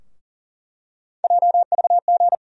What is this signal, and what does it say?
Morse code, heard as a keyed tone, about 38 wpm, 700 Hz. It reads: JVG
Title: USA Call Sign Postfixes - 3 Letters 40wpm